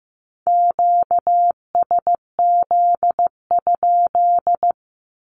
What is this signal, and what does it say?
Morse code, heard as a keyed tone, about 15 wpm, 700 Hz. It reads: QSZ?
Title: Q-Codes 15wpm